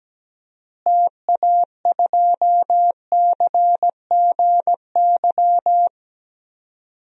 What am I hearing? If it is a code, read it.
Morse code, heard as a keyed tone, about 17 wpm, 700 Hz. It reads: TA2CGY
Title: DXCC-100 Call Signs - 6 Characters - Repeated 2x - 17wpm